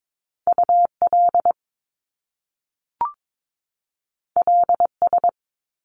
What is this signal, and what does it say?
Morse code, heard as a keyed tone, about 22 wpm, 700 Hz. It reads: UL LS
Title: Top Two Letter Combinations in Words 22wpm